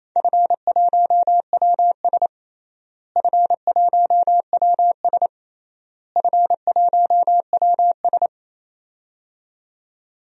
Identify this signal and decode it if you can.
Morse code, heard as a keyed tone, about 28 wpm, 700 Hz. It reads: F1WS F1WS F1WS
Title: DXCC-100 Call Signs - 4 Characters - Repeated 3x - 28wpm